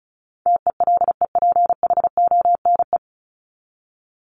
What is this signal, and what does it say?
Morse code, heard as a keyed tone, about 34 wpm, 700 Hz. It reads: TELEPHONE
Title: Top 1501-1750 Words 35wpm